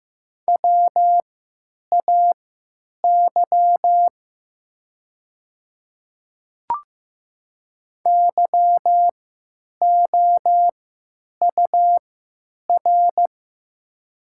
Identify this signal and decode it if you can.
Morse code, heard as a keyed tone, about 16 wpm, spaced as for 9 wpm, 700 Hz. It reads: WAY YOUR
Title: 3x Character Spacing - Top 100 Words - 15wpm